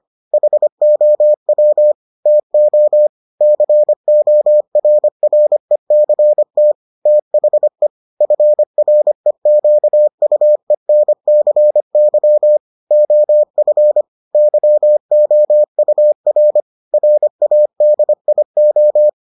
Morse code, heard as a keyed tone, about 25 wpm, 600 Hz. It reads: HOW TO CORRECT THE FREQUENCY OF YOUR RADIO